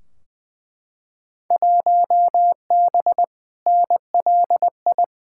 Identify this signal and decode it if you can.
Morse code, heard as a keyed tone, about 20 wpm, 700 Hz. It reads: 1B NLI
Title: Field Day Exchanges Encore - 20wpm